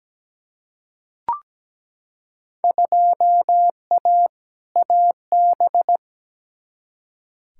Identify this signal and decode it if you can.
Morse code, heard as a keyed tone, about 17 wpm, 700 Hz. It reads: 2A AB
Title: Field Day Exchanges - 17wpm